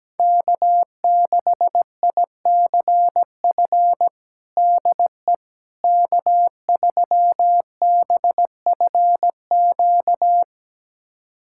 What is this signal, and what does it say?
Morse code, heard as a keyed tone, about 17 wpm, 700 Hz. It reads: K6ICF DE K3BFQ
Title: USA Callsign DE Callsign - 5 Characters - 17wpm